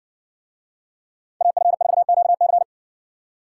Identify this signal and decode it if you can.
Morse code, heard as a keyed tone, about 60 wpm, 700 Hz. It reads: AF4QC